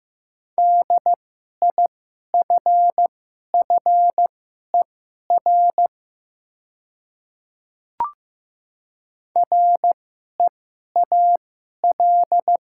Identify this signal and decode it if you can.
Morse code, heard as a keyed tone, about 15 wpm, spaced as for 11 wpm, 700 Hz. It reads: DIFFER REAL
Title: 2x Character Spacing - Top 300 Words - 15wpm